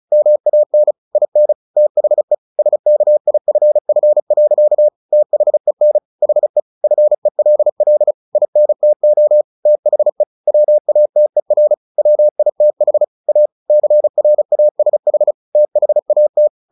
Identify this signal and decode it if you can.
Morse code, heard as a keyed tone, about 34 wpm, 600 Hz. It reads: MAN IN THE SKIFF. THEN HE FELL INTO THE WATER WITH A CRASH THAT
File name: the old man and the sea 2 35wpm